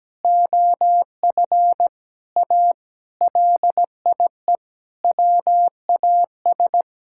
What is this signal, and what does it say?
Morse code, heard as a keyed tone, about 17 wpm, 700 Hz. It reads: OF A LIE WAS
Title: Sentences from Top 1600 Words 17wpm